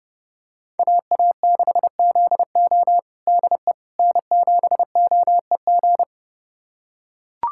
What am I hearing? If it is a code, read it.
Morse code, heard as a keyed tone, about 30 wpm, 700 Hz. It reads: AA6ZO DE N7OEG
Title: USA Callsign DE Callsign - 5 Characters - 30wpm